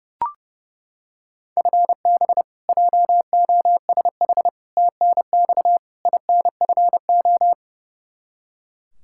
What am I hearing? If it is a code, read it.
Morse code, heard as a keyed tone, about 30 wpm, 700 Hz. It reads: FB JOSH TNX INFO
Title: Practice QSO - Exchange 3 - More Info - 30wpm